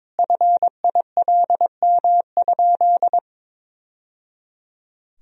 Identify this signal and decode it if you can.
Morse code, heard as a keyed tone, about 22 wpm, 700 Hz. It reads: FILM?
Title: Sentences from Top 1000 Words - Review Part 3 - 22wpm